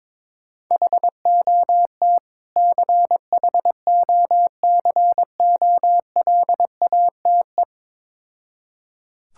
Morse code, heard as a keyed tone, about 21 wpm, 690 Hz. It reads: HOT CHOCOLATE
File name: Difficult Words with Hints - 22wpm